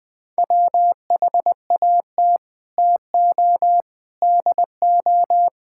Morse code, heard as a keyed tone, about 21 wpm, 700 Hz. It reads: WHAT TO DO
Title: Sets of 8 Words - 20wpm